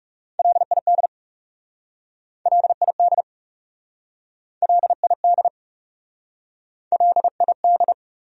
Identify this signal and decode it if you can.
Morse code, heard as a keyed tone, about 42 wpm, 700 Hz. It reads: LID LID LID LID